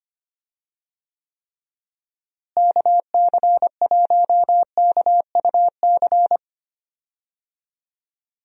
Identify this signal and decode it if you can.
Morse code, heard as a keyed tone, about 25 wpm, 700 Hz. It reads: KC1KUC
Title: USA Call Signs - 6 Characters - Repeated 3x - 25wpm